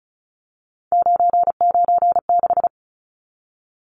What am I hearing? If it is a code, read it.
Morse code, heard as a keyed tone, about 34 wpm, 700 Hz. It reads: 996
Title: Numbers - 3-Digit 35wpm